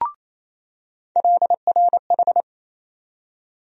Morse code, heard as a keyed tone, about 27 wpm, 700 Hz. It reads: LRH
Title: USA Call Sign Postfixes - 3 Letters 28wpm